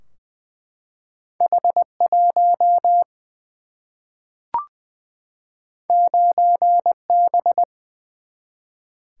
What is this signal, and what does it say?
Morse code, heard as a keyed tone, about 20 wpm, 700 Hz. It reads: H1 9B
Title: Single-Letter Single-Number 20wpm